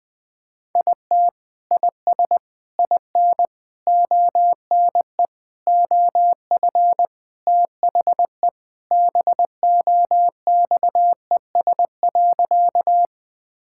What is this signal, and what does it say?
Morse code, heard as a keyed tone, about 21 wpm, 700 Hz. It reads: IT IS IN ONE OF THE BOXES.
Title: Sentences with 3-Word Phrases - 20wpm